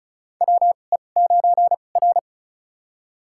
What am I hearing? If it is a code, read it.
Morse code, heard as a keyed tone, about 36 wpm, spaced as for 26 wpm, 690 Hz. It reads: WE9R
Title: USA Call Signs - 4 Characters - 2x Character Spacing - 35wpm